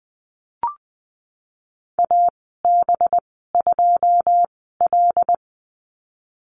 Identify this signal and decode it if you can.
Morse code, heard as a keyed tone, about 20 wpm, spaced as for 15 wpm, 700 Hz. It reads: AB2L
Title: 213 Lesson 34g - Number 8 - Call Signs 20wpm